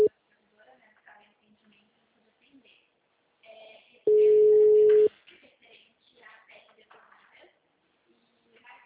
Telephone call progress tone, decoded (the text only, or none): ringback tone